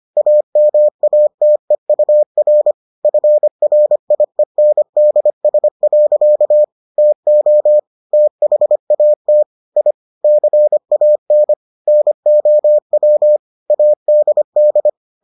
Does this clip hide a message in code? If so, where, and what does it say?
Yes: Morse code from 0.2 to 14.9 s, heard as a keyed tone, about 25 wpm, 600 Hz. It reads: AMATEUR FRIENDS. TO THAT I CAN NOW ADD